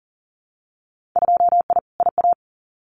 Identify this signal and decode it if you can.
Morse code, heard as a keyed tone, about 38 wpm, 700 Hz. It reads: 2I IA
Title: Winter Field Day Exchanges Encore 40wpm